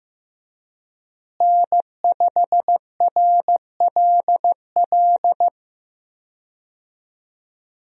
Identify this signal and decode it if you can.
Morse code, heard as a keyed tone, about 15 wpm, 700 Hz. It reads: N5RLL
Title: USA Call Signs - 5 Characters - Repeated 2x - 15wpm